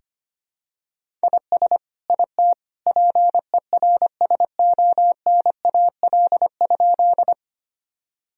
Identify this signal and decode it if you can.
Morse code, heard as a keyed tone, about 25 wpm, 700 Hz. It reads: IS IT PERSONAL?
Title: Sentences from Top 1000 Words - Review Part 4 - 25wpm